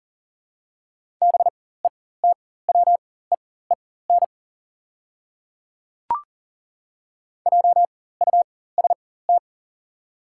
Morse code, heard as a keyed tone, about 40 wpm, spaced as for 19 wpm, 700 Hz. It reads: BETWEEN JUST
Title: 4x Character Spacing - Top 500 Words - 40wpm